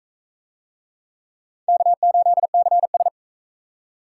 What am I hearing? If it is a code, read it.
Morse code, heard as a keyed tone, about 39 wpm, 700 Hz. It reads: K8CS